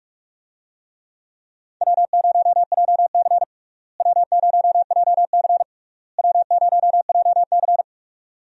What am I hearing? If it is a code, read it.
Morse code, heard as a keyed tone, about 41 wpm, 700 Hz. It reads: W0JC W0JC W0JC
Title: USA Call Signs - 4 Characters - Repeated 3x - 45wpm